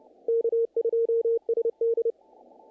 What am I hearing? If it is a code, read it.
Morse code, heard as a keyed tone, about 30 wpm, 500 Hz. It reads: K2SD
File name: contests / naqp-cw-k5tr-op-k5pi- -1800